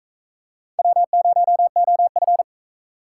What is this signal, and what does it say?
Morse code, heard as a keyed tone, about 41 wpm, 700 Hz. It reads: W0OF